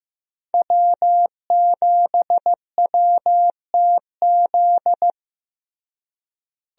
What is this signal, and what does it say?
Morse code, heard as a keyed tone, about 15 wpm, 700 Hz. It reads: W7WTZ
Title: USA Call Signs - 5 Characters - Repeated 3x - 15wpm